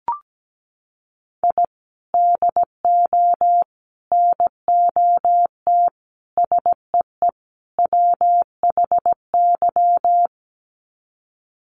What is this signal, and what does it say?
Morse code, heard as a keyed tone, about 17 wpm, 700 Hz. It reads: I DO NOT SEE WHY
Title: Sets of 5 Words Encore 17wpm